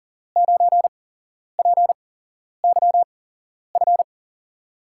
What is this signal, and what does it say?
Morse code, heard as a keyed tone, about 38 wpm, spaced as for 11 wpm, 700 Hz. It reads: 9PYF
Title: USA Call Signs - 6 Characters - 8x Character Spacing - 40wpm